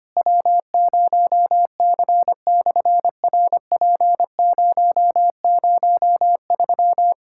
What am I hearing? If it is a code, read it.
Morse code, heard as a keyed tone, about 24 wpm, 700 Hz. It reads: W0C/RP003
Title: SOTA QSO Practice 25wpm